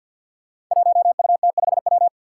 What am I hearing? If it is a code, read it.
Morse code, heard as a keyed tone, about 47 wpm, 700 Hz. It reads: 1UT5W
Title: Five Challenging Letter-Number Combinations 50wpm